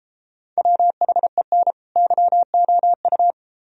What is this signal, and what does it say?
Morse code, heard as a keyed tone, about 34 wpm, 700 Hz. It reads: WHEN YOU
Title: Speed-Racing - Sets of 2 Words 28wpm